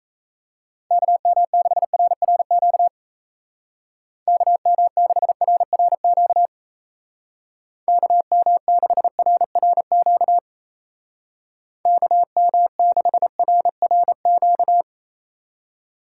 Morse code, heard as a keyed tone, about 35 wpm, 700 Hz. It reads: KM6RRQ KM6RRQ KM6RRQ KM6RRQ
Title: Speed Racing - USA Call Signs 28wpm